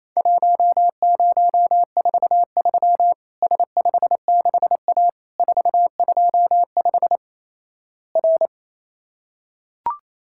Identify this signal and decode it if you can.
Morse code, heard as a keyed tone, about 28 wpm, 700 Hz. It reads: 1043 S56A 425 R
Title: Worked All Europe DX Contest - QTC Exchange - 28wpm